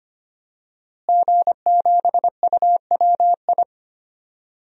Morse code, heard as a keyed tone, about 25 wpm, 700 Hz. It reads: G7UWI
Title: DXCC-100 Call Signs - 5 Characters - Repeated 3x - 25wpm